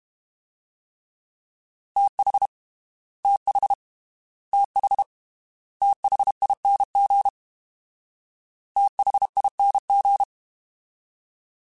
Morse code, heard as a keyed tone, about 32 wpm, 800 Hz. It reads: TH TH TH THING THING
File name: TH / TH mp3s / TH Prefixes 3x2 Twice @30